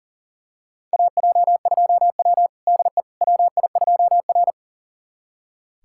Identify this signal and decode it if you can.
Morse code, heard as a keyed tone, about 40 wpm, 700 Hz. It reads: AJ2W DE WI2R
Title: USA Callsign DE Callsign - 4 Characters - 40wpm